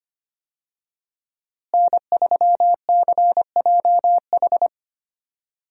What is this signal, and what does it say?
Morse code, heard as a keyed tone, about 25 wpm, 700 Hz. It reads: N3CJH
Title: USA Call Signs - 5 Characters - Repeated 2x - 25wpm